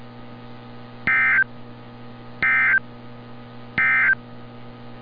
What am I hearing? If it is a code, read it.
Emergency Alert System (SAME) tones, heard as harsh buzzing bursts. It is heard from 1.0 to 4.2 s.
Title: NOAA Weather Radio - Altoona, PA